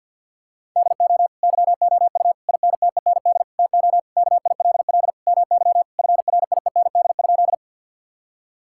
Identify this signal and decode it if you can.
Morse code, heard as a keyed tone, about 50 wpm, 700 Hz. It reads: DO YOU INTEND TO KILL MY FRIEND?
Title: Sentences from Top 1600 Words 50wpm